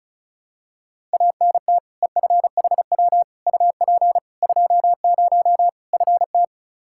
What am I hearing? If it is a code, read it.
Morse code, heard as a keyed tone, about 36 wpm, 700 Hz. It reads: ANT EFHW UP 20 FT
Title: Practice QSO - Exchange 2 - Info - 35wpm